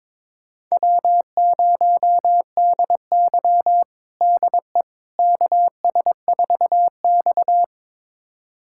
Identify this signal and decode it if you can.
Morse code, heard as a keyed tone, about 22 wpm, 700 Hz. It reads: W0DY DE KS4X
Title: USA Callsign DE Callsign - 4 Characters - 22wpm